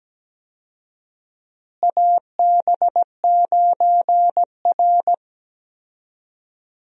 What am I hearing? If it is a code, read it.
Morse code, heard as a keyed tone, about 17 wpm, 700 Hz. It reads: AB9R
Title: USA Call Signs - 4 Characters - Repeated 3x - 17wpm